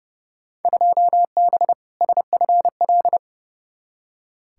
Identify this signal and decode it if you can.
Morse code, heard as a keyed tone, about 31 wpm, 700 Hz. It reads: 2B SFL
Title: Field Day Exchanges - 30wpm